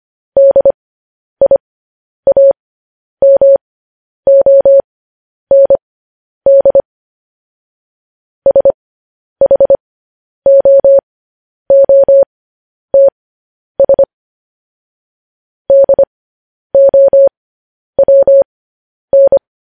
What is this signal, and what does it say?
Morse code, heard as a keyed tone, about 25 wpm, spaced as for 10 wpm, 550 Hz. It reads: DIAMOND SHOOTS DOWN